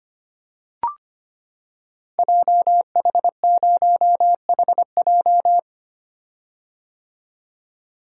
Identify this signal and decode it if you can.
Morse code, heard as a keyed tone, about 25 wpm, 690 Hz. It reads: JH0HJ
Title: Code Groups - 5 Letter:Numbers - 25wpm